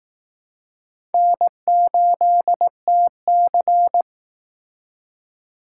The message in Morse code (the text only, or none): N8TC